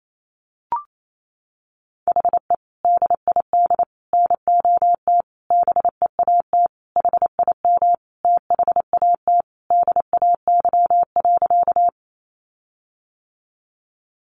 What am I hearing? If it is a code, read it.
Morse code, heard as a keyed tone, about 28 wpm, 700 Hz. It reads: HE DID NOT BEAT HIM THAT DAY.
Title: Sentences from Top 1300 Words 28wpm